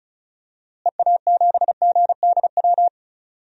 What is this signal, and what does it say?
Morse code, heard as a keyed tone, about 34 wpm, 700 Hz. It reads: EA7GDW
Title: DXCC-100 Call Signs - 6 Characters - Repeated 2x - 35wpm